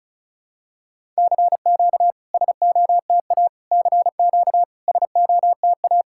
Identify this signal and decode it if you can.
Morse code, heard as a keyed tone, about 36 wpm, 690 Hz. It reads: CQ SOTA CQ SOTA
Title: SOTA QSO Practice 35wpm